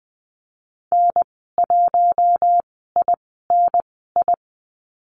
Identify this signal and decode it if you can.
Morse code, heard as a keyed tone, about 20 wpm, spaced as for 15 wpm, 700 Hz. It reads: N1INI
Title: 048 Lesson 9g - Number 4 - Call Signs 20wpm